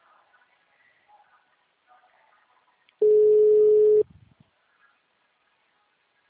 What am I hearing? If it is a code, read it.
Ringback tone (Europe, 425 Hz).
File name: out- -2001- - - .13376